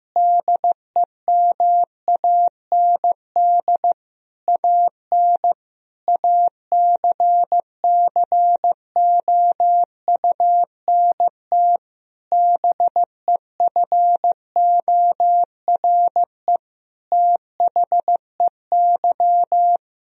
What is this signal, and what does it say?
Morse code, heard as a keyed tone, about 15 wpm, 700 Hz. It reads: DEMAND AN ACCOUNT BEFORE THEY
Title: Sentences from Top 1000 Words - Review Part 4 - 15wpm